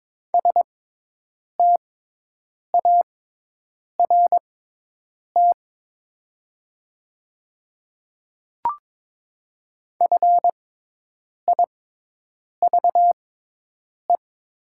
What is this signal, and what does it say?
Morse code, heard as a keyed tone, about 22 wpm, spaced as for 8 wpm, 700 Hz. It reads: START FIVE
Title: 6x Character Spacing - Top 500 Words - 22wpm